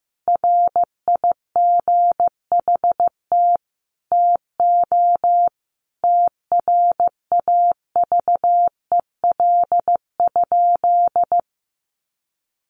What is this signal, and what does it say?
Morse code, heard as a keyed tone, about 15 wpm, 700 Hz. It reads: RIGHT TO TRAVEL?
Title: Sentences from Top 1400 Words 15wpm